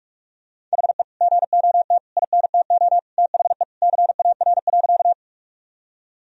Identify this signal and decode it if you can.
Morse code, heard as a keyed tone, about 42 wpm, 700 Hz. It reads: HE GOT INTO THE CAR.
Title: Sentences from Top 1000 Words - Review Part 1 - 45wpm